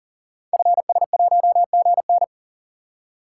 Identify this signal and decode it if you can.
Morse code, heard as a keyed tone, about 40 wpm, 690 Hz. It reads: FS1GN
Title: Code Groups - 5 Letter:Numbers - 40wpm